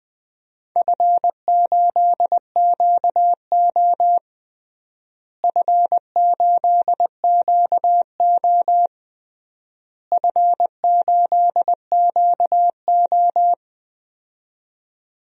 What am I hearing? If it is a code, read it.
Morse code, heard as a keyed tone, about 20 wpm, 700 Hz. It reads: F8QO F8QO F8QO
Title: DXCC-100 Call Signs - 4 Characters - Repeated 3x - 20wpm